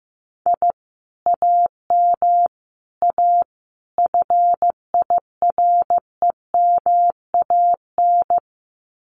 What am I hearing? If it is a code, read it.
Morse code, heard as a keyed tone, about 15 wpm, 700 Hz. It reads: I AM A FIREMAN
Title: Common QSO Phrases Encore 15wpm